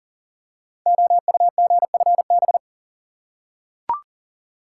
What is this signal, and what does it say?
Morse code, heard as a keyed tone, about 38 wpm, 690 Hz. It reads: OUGFB
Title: Code Groups - 5 Letters - 40wpm